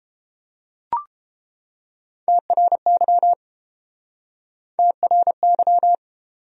Morse code, heard as a keyed tone, about 31 wpm, 700 Hz. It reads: TRY TRY